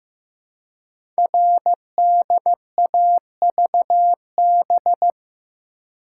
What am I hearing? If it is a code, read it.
Morse code, heard as a keyed tone, about 15 wpm, 700 Hz. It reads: RDAVB
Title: Code Groups - 5 Letters - 15wpm